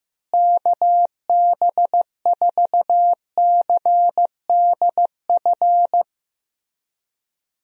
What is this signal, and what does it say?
Morse code, heard as a keyed tone, about 15 wpm, 700 Hz. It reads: KB4CDF
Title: USA Call Signs - 6 Characters - Repeated 2x - 15wpm